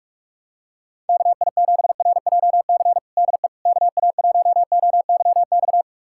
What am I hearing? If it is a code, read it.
Morse code, heard as a keyed tone, about 46 wpm, 700 Hz. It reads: KI7RJC DE KA1OYX
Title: USA Callsign DE Callsign - 6 Characters - 45wpm